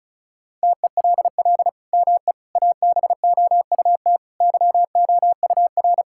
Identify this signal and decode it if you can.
Morse code, heard as a keyed tone, about 36 wpm, 700 Hz. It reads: TELL ME ABOUT YOUR
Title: Sentences from Top 1000 Words - Review Part 6 - 35wpm